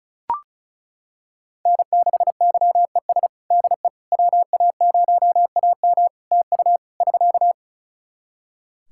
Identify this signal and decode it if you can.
Morse code, heard as a keyed tone, about 36 wpm, 700 Hz. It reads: N6YES DE WA0AM TU <SK>
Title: Practice QSO - Exchange 4 - Ending - 35wpm